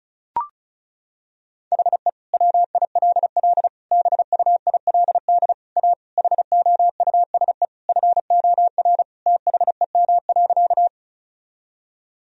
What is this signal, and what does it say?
Morse code, heard as a keyed tone, about 34 wpm, 700 Hz. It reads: HE WILL BUILD A HOUSE FOR THEM.
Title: Sentences from Top 1000 Words - Review Part 1 - 35wpm